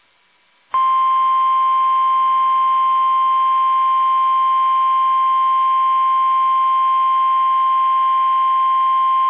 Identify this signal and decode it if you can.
Emergency Alert System (SAME) tones (the 1050 Hz attention tone). It is heard from 0.9 to 8.4 s.
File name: eas / KUSO / KUSO